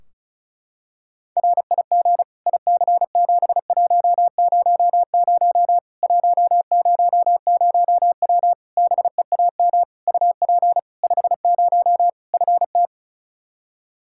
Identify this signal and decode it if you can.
Morse code, heard as a keyed tone, about 36 wpm, 700 Hz. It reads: RIG IC7100 100W BEAM UP 50 FT
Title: Practice QSO - Exchange 2 - Info - 35wpm